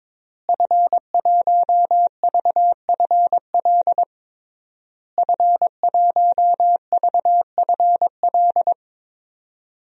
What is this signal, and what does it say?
Morse code, heard as a keyed tone, about 21 wpm, 700 Hz. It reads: F1VFL F1VFL
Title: DXCC-100 Call Signs - 5 Characters - Repeated 3x - 22wpm